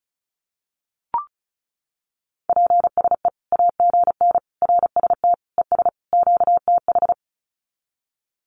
Morse code, heard as a keyed tone, about 34 wpm, 700 Hz. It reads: PSE AGN RST ES QTH
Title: Common QSO Phrases 35wpm